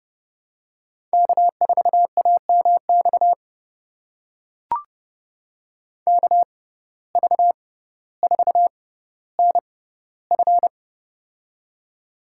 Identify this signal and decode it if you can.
Morse code, heard as a keyed tone, about 32 wpm, spaced as for 16 wpm, 700 Hz. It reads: K4AMX KV4NF
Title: USA Call Signs - 6x Character Spacing - 30wpm